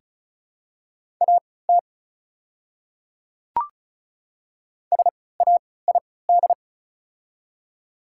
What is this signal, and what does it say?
Morse code, heard as a keyed tone, about 34 wpm, spaced as for 20 wpm, 700 Hz. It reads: AT SAID
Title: 3x Character Spacing - Top 300 Words - 35wpm